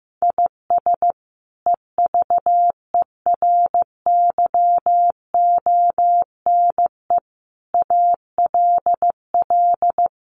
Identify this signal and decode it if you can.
Morse code, heard as a keyed tone, about 15 wpm, 700 Hz. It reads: IS EVERYONE ALL
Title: Sentences from Top 800 Words 15wpm